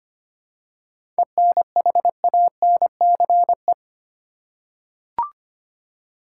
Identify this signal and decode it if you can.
Morse code, heard as a keyed tone, about 25 wpm, 700 Hz. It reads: ENHANCE
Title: Seven Letter Words - 25wpm